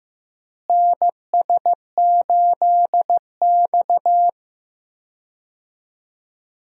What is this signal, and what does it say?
Morse code, heard as a keyed tone, about 16 wpm, 700 Hz. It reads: NS8X
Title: USA Call Signs - 4 Characters - Repeated 2x - 15wpm